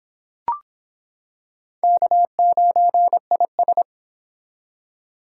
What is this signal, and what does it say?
Morse code, heard as a keyed tone, about 25 wpm, 700 Hz. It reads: K9IS